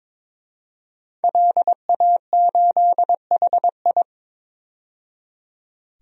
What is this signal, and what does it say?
Morse code, heard as a keyed tone, about 21 wpm, 700 Hz. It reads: LA8HI
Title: DXCC-100 Call Signs - 5 Characters - 22wpm